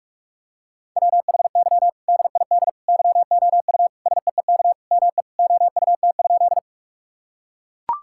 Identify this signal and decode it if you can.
Morse code, heard as a keyed tone, about 42 wpm, 700 Hz. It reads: WHY DID YOU SEEK ME OUT?
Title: Sentences from Top 1600 Words 45wpm